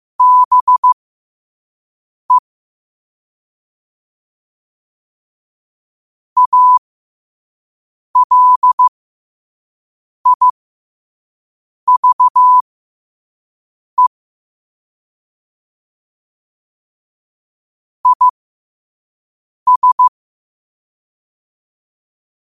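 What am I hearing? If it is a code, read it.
Morse code, heard as a keyed tone, about 15 wpm, spaced as for 5 wpm, 1000 Hz. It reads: BE ALIVE IS
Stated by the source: Morse code quote practice at 5 words per minute.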